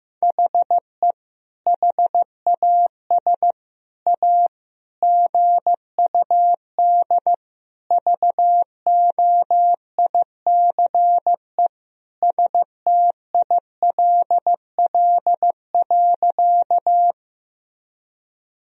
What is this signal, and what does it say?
Morse code, heard as a keyed tone, about 15 wpm, 700 Hz. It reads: HE HAS A GUD VOICE STILL.
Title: Sentences from Top 500 Words and Abbreviations 15wpm